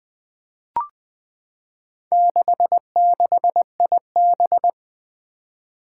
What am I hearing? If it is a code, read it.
Morse code, heard as a keyed tone, about 20 wpm, 700 Hz. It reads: 66IB
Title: Four Challenging Letter-Number Combinations 20wpm